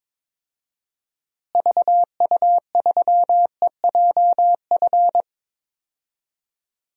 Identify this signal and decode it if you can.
Morse code, heard as a keyed tone, about 22 wpm, 700 Hz. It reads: VU3EJF
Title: DXCC-100 Call Signs - 6 Characters - Repeated 2x - 22wpm